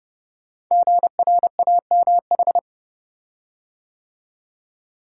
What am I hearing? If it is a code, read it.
Morse code, heard as a keyed tone, about 30 wpm, 700 Hz. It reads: GRAMH
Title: Code Groups - 5 Letters - 30wpm